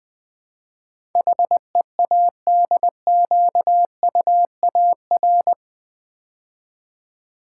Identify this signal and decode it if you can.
Morse code, heard as a keyed tone, about 20 wpm, 700 Hz. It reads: HEADQUAR
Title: Build A Word - 4 Letters At A Time - 12-Letter Words - 20wpm